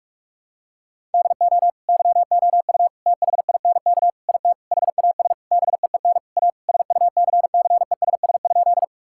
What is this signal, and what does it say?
Morse code, heard as a keyed tone, about 42 wpm, 700 Hz. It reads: DO YOU THINK IT HAS BEEN A SUCCESS?
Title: Sentences from Top 1000 Words - Review Part 6 - 45wpm